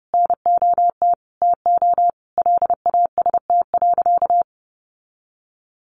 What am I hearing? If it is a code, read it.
Morse code, heard as a keyed tone, about 30 wpm, 700 Hz. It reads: NOT TO LAST.
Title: Sentences from Top 1500 Words 30wpm